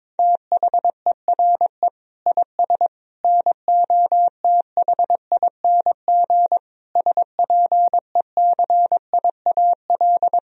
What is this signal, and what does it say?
Morse code, heard as a keyed tone, about 22 wpm, 700 Hz. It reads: THERE IS NOTHING SPECIAL
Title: Sentences from Top 1000 Words - Review Part 3 - 22wpm